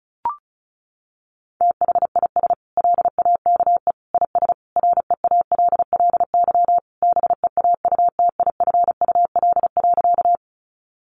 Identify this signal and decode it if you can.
Morse code, heard as a keyed tone, about 34 wpm, 700 Hz. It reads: THIS LAKE IS REALLY BEAUTIFUL.
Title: Sentences from Top 1100 Words 35wpm